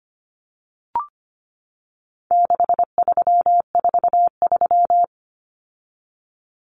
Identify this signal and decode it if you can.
Morse code, heard as a keyed tone, about 25 wpm, 700 Hz. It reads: 6343
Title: Numbers - 4-Digit 25wpm